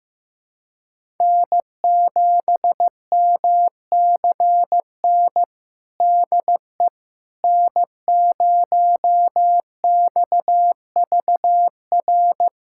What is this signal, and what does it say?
Morse code, heard as a keyed tone, about 16 wpm, 700 Hz. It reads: N7MCN DE N0XVR
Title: USA Callsign DE Callsign - 5 Characters - 15wpm